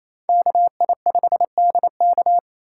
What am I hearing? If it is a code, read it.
Morse code, heard as a keyed tone, about 36 wpm, 700 Hz. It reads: KI5DK